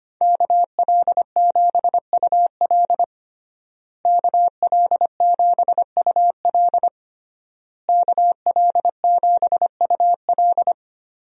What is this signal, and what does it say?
Morse code, heard as a keyed tone, about 25 wpm, 700 Hz. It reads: KL7UL KL7UL KL7UL
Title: USA Call Signs - 5 Characters - Repeated 3x - 25wpm